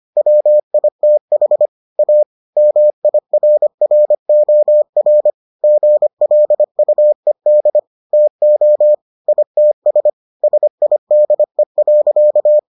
Morse code, heard as a keyed tone, about 25 wpm, 600 Hz. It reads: WITH A MIRROR GLUED TO ITS SIDE.